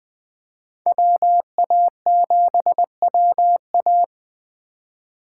Morse code, heard as a keyed tone, about 20 wpm, 700 Hz. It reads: WA7WA
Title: USA Call Signs - 5 Characters - 2x Character Spacing - 20wpm